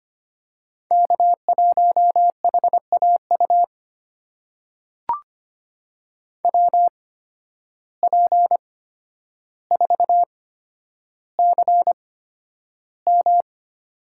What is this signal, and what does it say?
Morse code, heard as a keyed tone, about 25 wpm, spaced as for 11 wpm, 700 Hz. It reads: K1HAU WP4CM
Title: USA Call Signs - 8x Character Spacing - 25wpm